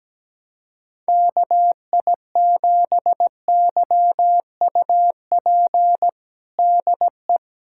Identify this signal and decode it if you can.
Morse code, heard as a keyed tone, about 17 wpm, 700 Hz. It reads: KI7YUP DE
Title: USA Callsign DE Callsign 17wpm